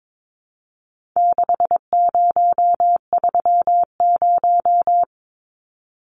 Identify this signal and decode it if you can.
Morse code, heard as a keyed tone, about 22 wpm, 700 Hz. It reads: 6030
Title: Numbers - 4-Digit 22wpm